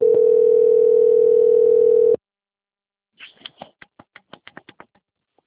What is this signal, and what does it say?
Ringback tone (North America), 440 + 480 Hz.